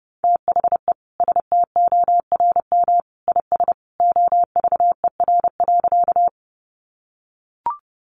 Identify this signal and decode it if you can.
Morse code, heard as a keyed tone, about 30 wpm, 700 Hz. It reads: THE STORM IS OVER.
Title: Sentences from Top 1500 Words 30wpm